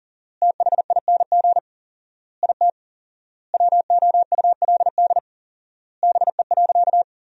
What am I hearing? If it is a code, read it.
Morse code, heard as a keyed tone, about 40 wpm, 700 Hz. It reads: THING IT WOULD BE.
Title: 4x Word-Spacing - Sentences from Top 200 Words - 40wpm